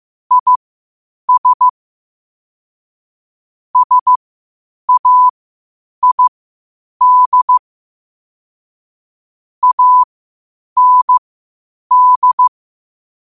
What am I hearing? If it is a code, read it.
Morse code, heard as a keyed tone, about 15 wpm, spaced as for 8 wpm, 1000 Hz. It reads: IS SAID AND
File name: 8wpm